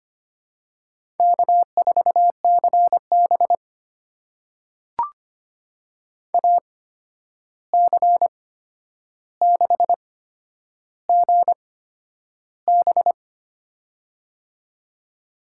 Morse code, heard as a keyed tone, about 25 wpm, spaced as for 10 wpm, 700 Hz. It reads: K4CB AC6GB
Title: USA Call Signs - 8x Character Spacing - 25wpm